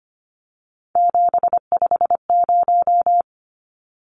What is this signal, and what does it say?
Morse code, heard as a keyed tone, about 25 wpm, 700 Hz. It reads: 750
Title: Numbers - 3-Digit 25wpm